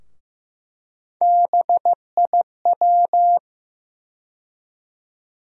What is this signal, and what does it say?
Morse code, heard as a keyed tone, about 15 wpm, 700 Hz. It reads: BIW
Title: USA Call Sign Postfixes - 3 Letters 15wpm